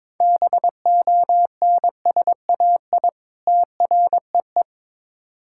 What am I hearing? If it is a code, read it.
Morse code, heard as a keyed tone, about 22 wpm, 700 Hz. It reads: BONSAI TREE
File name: Difficult Words with Hints - 22wpm